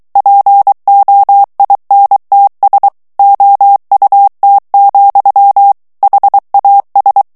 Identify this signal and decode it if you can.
Morse code, heard as a keyed tone, about 23 wpm, 800 Hz. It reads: POINTS OUT, HAS
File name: ANGLAIS texte clair 3 vitesse 22 mots minute